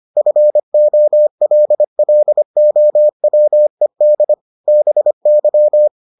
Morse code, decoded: FOLLOWED BY

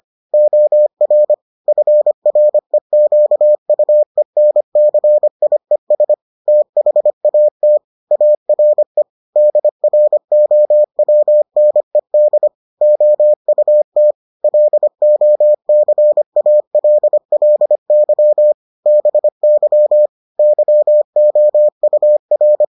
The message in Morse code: OR FREQUENCIES THAT ARE DROWNED OUT LOCALLY BY YOUR

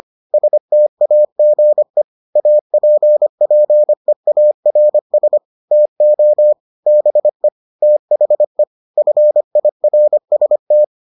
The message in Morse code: STAGE APPEARS TO BE THE FIRST

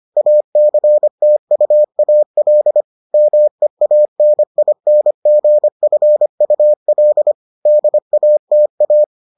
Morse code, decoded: ACTUAL MEANINGFUL DATA

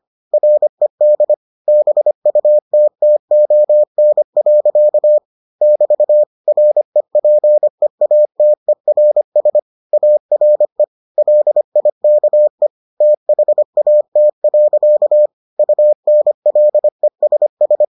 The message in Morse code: RED BUTTON. <BT> REPEATERS ARE LIKE THAT. UNLESS

